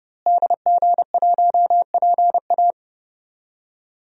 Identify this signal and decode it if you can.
Morse code, heard as a keyed tone, about 30 wpm, 700 Hz. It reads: DG1PA
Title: DXCC-100 Call Signs - 5 Characters - Repeated 2x - 30wpm